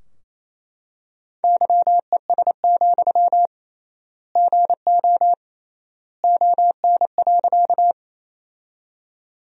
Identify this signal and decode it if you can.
Morse code, heard as a keyed tone, about 28 wpm, 700 Hz. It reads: YES, GO ON.
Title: 3x Word-Spacing - Sentences from Top 500 Words - 28wpm